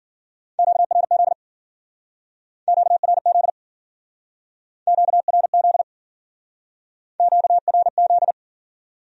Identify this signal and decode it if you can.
Morse code, heard as a keyed tone, about 52 wpm, 700 Hz. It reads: QRZ QRZ QRZ QRZ